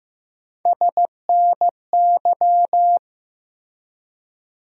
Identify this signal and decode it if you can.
Morse code, heard as a keyed tone, about 15 wpm, 700 Hz. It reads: SNY
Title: USA Call Sign Postfixes - 3 Letters 15wpm